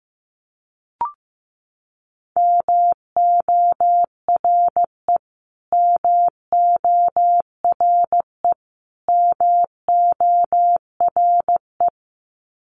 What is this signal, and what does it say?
Morse code, heard as a keyed tone, about 15 wpm, 700 Hz. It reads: MORE MORE MORE
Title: Top 200 Words Repeated 15wpm